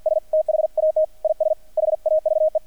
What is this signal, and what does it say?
Morse code, heard as a keyed tone, about 50 wpm, 630 Hz. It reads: S THAT IS HAVE
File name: archived / CRSnet